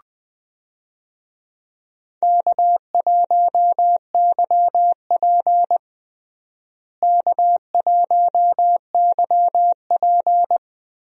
Morse code, heard as a keyed tone, about 21 wpm, 700 Hz. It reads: K1YP K1YP
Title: USA Call Signs - 4 Characters - Repeated 3x - 20wpm